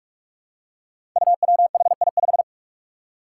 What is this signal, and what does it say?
Morse code, heard as a keyed tone, about 44 wpm, 700 Hz. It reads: UWHI5
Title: Five Challenging Letter-Number Combinations 45wpm